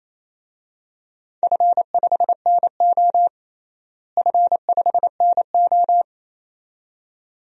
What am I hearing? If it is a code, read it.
Morse code, heard as a keyed tone, about 28 wpm, 700 Hz. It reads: F5NO F5NO
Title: DXCC-100 Call Signs - 4 Characters - Repeated 2x - 28wpm